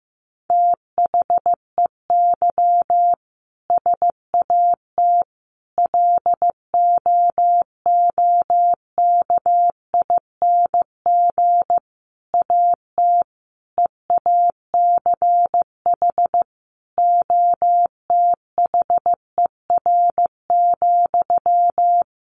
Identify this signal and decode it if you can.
Morse code, heard as a keyed tone, about 15 wpm, 700 Hz. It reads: THEY SAT LOOKING AT EACH OTHER,
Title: A Sound Of Thunder 15wpm